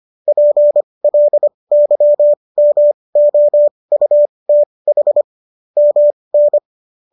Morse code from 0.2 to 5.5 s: PLYMOUTH

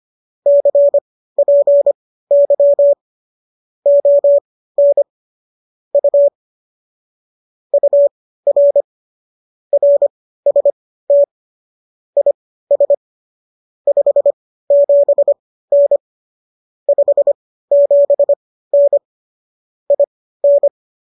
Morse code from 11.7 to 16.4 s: IS 57N

